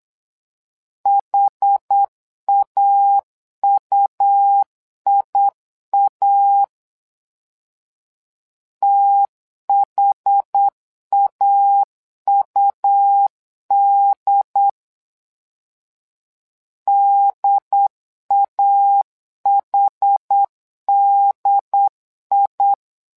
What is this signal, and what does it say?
Morse code, heard as a keyed tone, about 8 wpm, 800 Hz. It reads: HAUIA THAUD DAHDI
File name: LEZIONI / 2 UHDN / 2 UHD 8wpm 2258